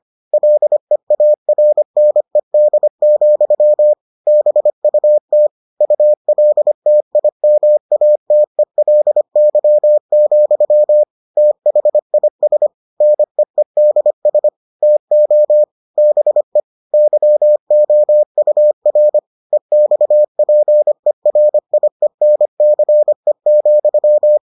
Morse code, heard as a keyed tone, about 25 wpm, 600 Hz. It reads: LEARNED, BUT ULTIMATELY, THIS NEEDS TO BE YOUR EXPERIENCE,